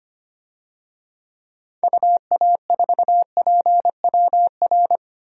Morse code, heard as a keyed tone, about 25 wpm, 700 Hz. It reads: UA4PWR
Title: DXCC-100 Call Signs - 6 Characters - 25wpm